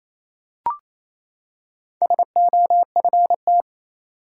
Morse code, heard as a keyed tone, about 28 wpm, 700 Hz. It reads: SOFT
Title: Four Letter Words - 28wpm